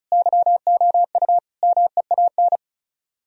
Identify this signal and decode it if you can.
Morse code, heard as a keyed tone, about 45 wpm, 700 Hz. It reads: YOU MEAN